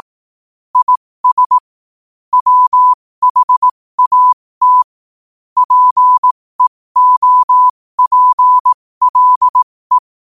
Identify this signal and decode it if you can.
Morse code, heard as a keyed tone, about 18 wpm, spaced as for 15 wpm, 1000 Hz. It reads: IS WHAT PEOPLE
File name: qotdfiles / 15wpm